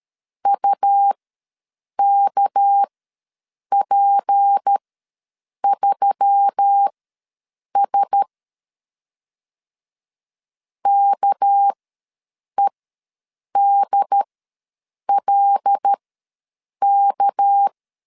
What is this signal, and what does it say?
Morse code, heard as a keyed tone, about 13 wpm, spaced as for 7 wpm, 800 Hz. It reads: UKP3S KEDLK
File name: LEZIONI / 6 PKX0 / 6 PK ws 12wpm 3116